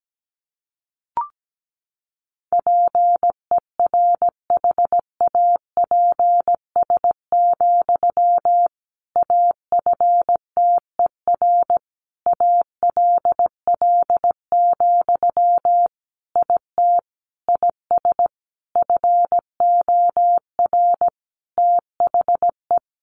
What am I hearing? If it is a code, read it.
Morse code, heard as a keyed tone, about 17 wpm, 700 Hz. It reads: PERHAPS, AFTER ALL, IT IS FOR THE
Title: Sentences from Top 500 Words 17wpm